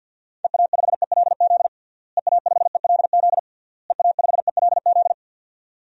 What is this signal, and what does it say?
Morse code, heard as a keyed tone, about 51 wpm, 700 Hz. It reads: EA5ELZ EA5ELZ EA5ELZ
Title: DXCC-100 Call Signs - 6 Characters - Repeated 3x - 50wpm